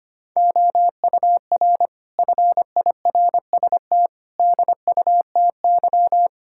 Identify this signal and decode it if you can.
Morse code, heard as a keyed tone, about 25 wpm, 700 Hz. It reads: OUR FIRST DUTY
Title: Sentences from Top 1600 Words 25wpm